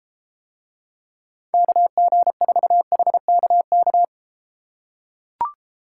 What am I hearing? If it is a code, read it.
Morse code, heard as a keyed tone, about 32 wpm, 700 Hz. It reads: KG4HKK